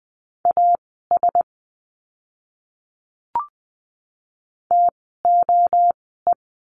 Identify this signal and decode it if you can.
Morse code, heard as a keyed tone, about 20 wpm, spaced as for 15 wpm, 700 Hz. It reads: AS TOE
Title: 036 Lesson 7f - Letter s - Words 20wpm